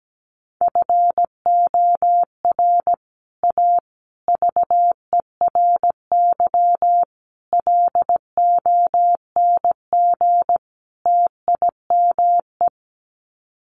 Morse code, heard as a keyed tone, about 17 wpm, 700 Hz. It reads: FOR A VERY LONG TIME
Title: Sets of 5 Words 17wpm